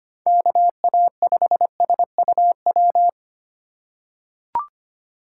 Morse code, heard as a keyed tone, about 25 wpm, 700 Hz. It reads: KA5SUW
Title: USA Call Signs - 2x Character Spacing - 25wpm